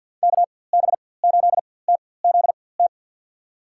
Morse code, heard as a keyed tone, about 52 wpm, spaced as for 24 wpm, 700 Hz. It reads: KB8TZT
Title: USA Call Signs - 6 Characters - 4x Character Spacing - 50wpm